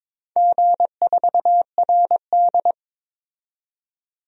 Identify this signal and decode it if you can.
Morse code, heard as a keyed tone, about 22 wpm, 700 Hz. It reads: G4RD
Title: DXCC-100 Call Signs - 4 Characters - 22wpm